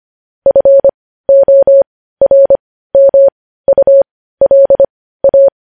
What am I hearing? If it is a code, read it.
Morse code, heard as a keyed tone, about 25 wpm, spaced as for 15 wpm, 550 Hz. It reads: FORMULA
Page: news-headlines-in-morse-code-at-15-wpm